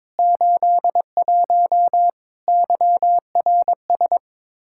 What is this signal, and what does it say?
Morse code, heard as a keyed tone, about 21 wpm, 700 Hz. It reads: 81 YRS
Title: Practice QSO - Exchange 3 - More Info - 22wpm